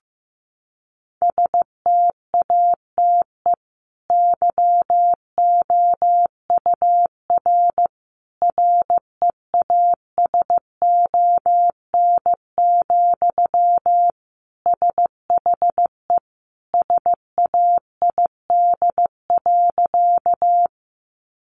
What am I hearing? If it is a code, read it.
Morse code, heard as a keyed tone, about 15 wpm, 700 Hz. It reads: STATE YOUR REASON, SHE SAID.
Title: Sentences from Top 700 Words 15wpm